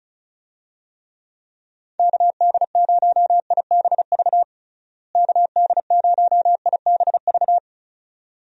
Morse code, heard as a keyed tone, about 36 wpm, 700 Hz. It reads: KD0IBV KD0IBV
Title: USA Call Signs - 6 Characters - Repeated 2x - 35wpm